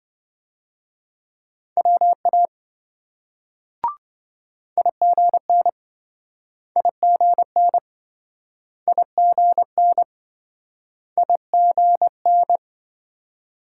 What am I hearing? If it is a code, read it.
Morse code, heard as a keyed tone, about 27 wpm, 700 Hz. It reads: WA IGN IGN IGN IGN